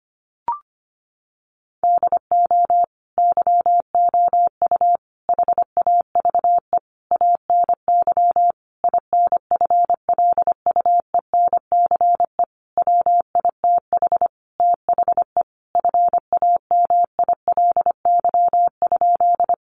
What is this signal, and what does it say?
Morse code, heard as a keyed tone, about 25 wpm, 700 Hz. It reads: DO YOU HAVE ANY INFLUENCE WITH THE FAMILY?
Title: Sentences from Top 1100 Words 25wpm